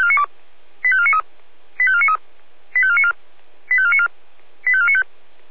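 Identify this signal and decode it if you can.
Five-tone selective call (ZVEI), heard as quick runs of five tones, tones of about 70 ms. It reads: 85493 85493 85494 85494 85495
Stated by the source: ZVEI five-tone selective call received on an SDR.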